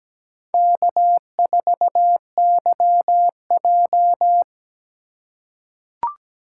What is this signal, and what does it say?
Morse code, heard as a keyed tone, about 17 wpm, 700 Hz. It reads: K4YJ
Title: USA Call Signs - 4 Characters - 3x Character Spacing - 17wpm